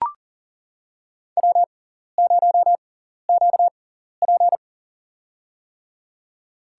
Morse code, heard as a keyed tone, about 40 wpm, spaced as for 14 wpm, 700 Hz. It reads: W0QP
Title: USA Call Signs - 4 Characters - 6x Character Spacing - 40wpm